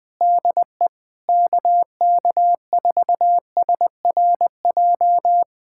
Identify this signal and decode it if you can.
Morse code, heard as a keyed tone, about 20 wpm, 700 Hz. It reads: DE KK4SRJ
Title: Speed-Racing Common QSO Phrases Encore - 17wpm